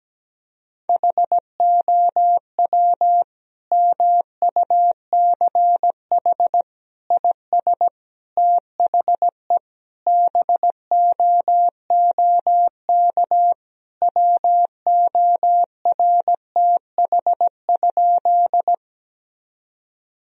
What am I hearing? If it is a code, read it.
Morse code, heard as a keyed tone, about 17 wpm, 700 Hz. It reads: HOW MUCH IS THE BOOK WORTH?
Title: Sentences from Top 1000 Words - Review Part 1 - 17wpm